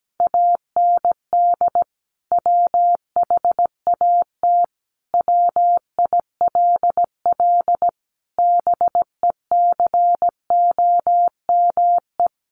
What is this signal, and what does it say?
Morse code, heard as a keyed tone, about 17 wpm, 700 Hz. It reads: AND WHAT WILL BECOME
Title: Sentences from Top 600 Words 17wpm